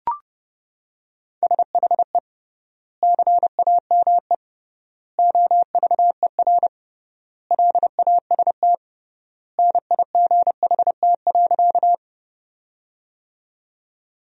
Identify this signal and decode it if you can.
Morse code, heard as a keyed tone, about 30 wpm, 700 Hz. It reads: SHE CAME OVER LAST NIGHT.
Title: 3x Word-Spacing - Sentences from Top 300 Words - 30wpm